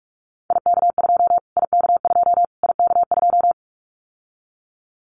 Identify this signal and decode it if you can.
Morse code, heard as a keyed tone, about 44 wpm, 700 Hz. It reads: IK2 IK2 IK2
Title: Callsign Prefixes for Top 25 DXCC - Repeated 3x - 45wpm